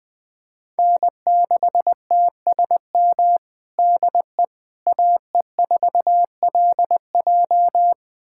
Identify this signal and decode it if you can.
Morse code, heard as a keyed tone, about 20 wpm, 700 Hz. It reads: N6TSM DE AE4LJ
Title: USA Callsign DE Callsign - 5 Characters - 20wpm